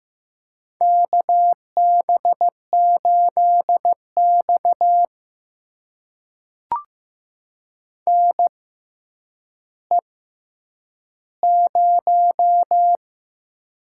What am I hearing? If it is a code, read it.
Morse code, heard as a keyed tone, about 15 wpm, spaced as for 9 wpm, 700 Hz. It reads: KB8X NE0
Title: USA Call Signs - 4 Characters - 6x Character Spacing - 15wpm